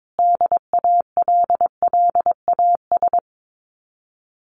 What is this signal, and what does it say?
Morse code, heard as a keyed tone, about 22 wpm, 700 Hz. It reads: DALLAS
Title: Common QSO Phrases Encore 22wpm